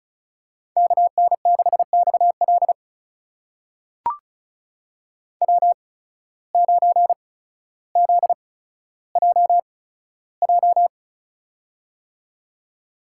Morse code, heard as a keyed tone, about 34 wpm, spaced as for 15 wpm, 700 Hz. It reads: KN6XL W9ZJJ
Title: USA Call Signs - 5 Characters - 8x Character Spacing - 35wpm